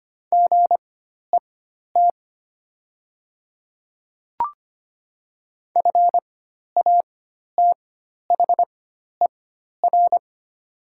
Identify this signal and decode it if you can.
Morse code, heard as a keyed tone, about 25 wpm, spaced as for 12 wpm, 690 Hz. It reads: GET FATHER
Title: 4x Character Spacing - Top 200 Words - 25wpm